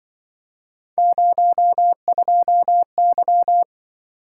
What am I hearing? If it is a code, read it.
Morse code, heard as a keyed tone, about 24 wpm, 700 Hz. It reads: 02Y
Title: Speed-Racing - Three Letter-Number Combinations - 20wpm